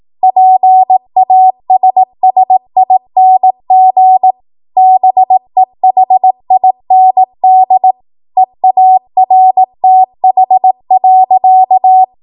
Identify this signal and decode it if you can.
Morse code, heard as a keyed tone, about 18 wpm, 750 Hz. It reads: PASSING BEHIND EARTH.
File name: Morse / 18WPM